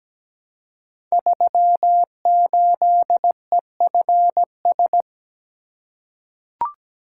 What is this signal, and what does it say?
Morse code, heard as a keyed tone, about 17 wpm, 700 Hz. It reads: 38EFS
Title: Code Groups - 5 Letter:Numbers - 17wpm